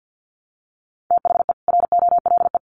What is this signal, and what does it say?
Morse code, heard as a keyed tone, about 50 wpm, 700 Hz. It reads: THE ROLE
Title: Sets of 4 Words 50wpm